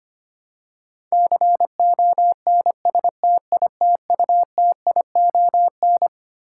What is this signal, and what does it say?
Morse code, heard as a keyed tone, about 25 wpm, 700 Hz. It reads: CONSTITUTION
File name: Twelve Letter Words - 25wpm